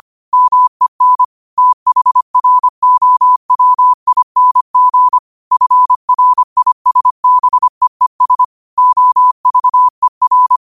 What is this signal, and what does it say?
Morse code, heard as a keyed tone, about 25 wpm, 1000 Hz. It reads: MEN THROWING FRISBEES OVER
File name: qotdfiles / 25wpm